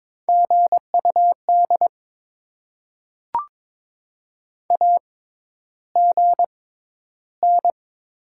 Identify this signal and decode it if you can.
Morse code, heard as a keyed tone, about 22 wpm, spaced as for 11 wpm, 690 Hz. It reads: GUD AGN
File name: Top 20 QSO Elements - 6x Character Spacing - 22wpm